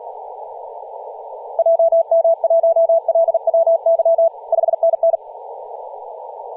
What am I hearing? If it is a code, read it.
Morse code, heard as a keyed tone, about 37 wpm, 660 Hz, with radio noise. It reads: JM1LWY 5NN
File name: posts / 2016-aug-26-2236z-yj0cow-18-073mhz-cw